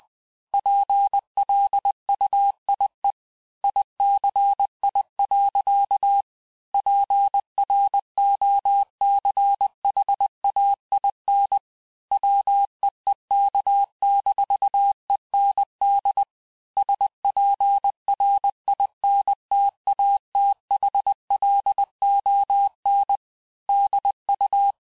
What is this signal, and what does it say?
Morse code, heard as a keyed tone, about 20 wpm, 800 Hz. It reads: PLUIE ICI. PROCHAIN WEEK-END SPRINTATHLON DU
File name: cours-du-mercredi / audios / cours-870-FIN-de-QSO-vitesse-19-MPM